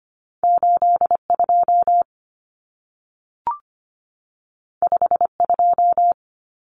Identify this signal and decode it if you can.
Morse code, heard as a keyed tone, about 25 wpm, 700 Hz. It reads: 82 52
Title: Common Cut Numbers - 2-Digit 25wpm